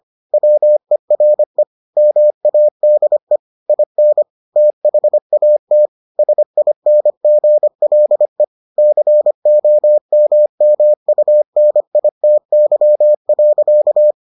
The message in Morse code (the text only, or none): WERE MADE IN THAT SINGLE COMMUNITY.